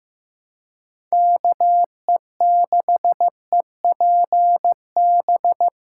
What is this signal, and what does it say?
Morse code, heard as a keyed tone, about 18 wpm, 700 Hz. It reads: KE6EPB
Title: Speed Racing - USA 2x3 Call Signs 15wpm